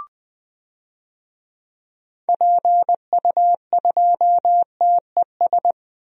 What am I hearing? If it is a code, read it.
Morse code, heard as a keyed tone, about 20 wpm, 700 Hz. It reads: PU2TES
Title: DXCC-100 Call Signs - 6 Characters - 20wpm